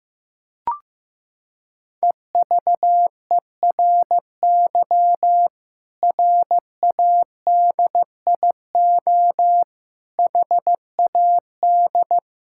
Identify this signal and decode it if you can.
Morse code, heard as a keyed tone, about 15 wpm, 700 Hz. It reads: EVERY RADIO HAD
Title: Sentences from Top 1000 Words - Review Part 6 - 15wpm